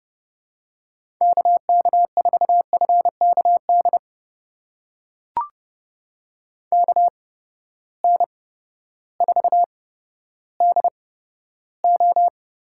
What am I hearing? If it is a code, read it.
Morse code, heard as a keyed tone, about 30 wpm, spaced as for 14 wpm, 700 Hz. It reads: KK4FKD KN4DO
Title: USA Call Signs - 6 Characters - 8x Character Spacing - 30wpm